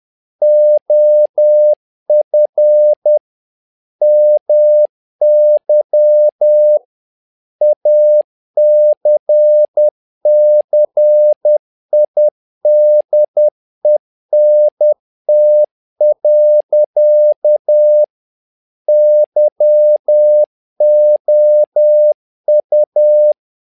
Morse code, heard as a keyed tone, about 10 wpm, 600 Hz. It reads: OF MY ACCIDENT. YOU
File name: learning center / tower trouble / tower trouble 10wpm